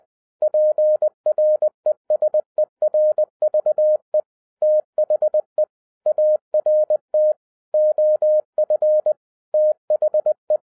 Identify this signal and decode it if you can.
Morse code, heard as a keyed tone, about 20 wpm, 610 Hz. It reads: PRESERVE THE ART OF THE